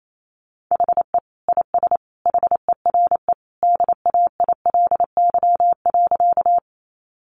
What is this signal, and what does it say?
Morse code, heard as a keyed tone, about 28 wpm, 700 Hz. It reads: HE IS HERE DAILY.
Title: Sentences from Top 1100 Words 28wpm